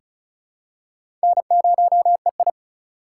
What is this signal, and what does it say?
Morse code, heard as a keyed tone, about 36 wpm, 700 Hz. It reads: N0EI
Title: USA Call Signs - 4x Character Spacing - 35wpm